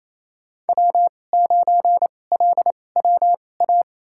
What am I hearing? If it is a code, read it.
Morse code, heard as a keyed tone, about 29 wpm, spaced as for 21 wpm, 700 Hz. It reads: W9LWA
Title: USA Call Signs - 5 Characters - 2x Character Spacing - 28wpm